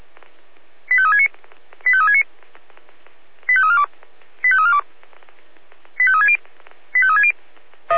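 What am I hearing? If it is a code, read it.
Five-tone selective call (ZVEI), heard as quick runs of five tones, tones of about 70 ms. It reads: 85369 85369 85342 85342 85360 85360
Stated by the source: ZVEI five-tone selective call 85349 received by SDR.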